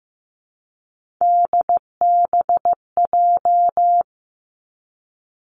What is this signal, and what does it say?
Morse code, heard as a keyed tone, about 15 wpm, 700 Hz. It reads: DBJ
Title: Three Challenging Letter-Number Combinations 15wpm